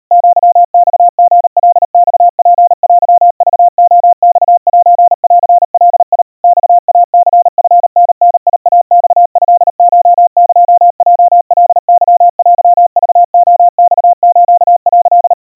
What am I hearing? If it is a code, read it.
Morse code (Japanese Wabun code), heard as a keyed tone, about 39 wpm, 700 Hz. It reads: アマリカマツテクレマセンガ マイニチタダイマトコエヲカケテクレマス」
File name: kawa / Morse sample / kamyu 140 cpm 700 Hz Crispy